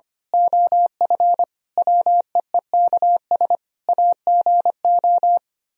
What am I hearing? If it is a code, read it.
Morse code, heard as a keyed tone, about 24 wpm, 700 Hz. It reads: OF WEEKS AGO
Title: Speed-Racing - Sets of 5 Words 25wpm